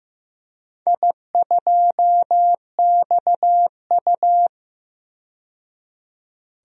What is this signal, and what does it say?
Morse code, heard as a keyed tone, about 16 wpm, 700 Hz. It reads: I2XU
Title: DXCC-100 Call Signs - 4 Characters - 15wpm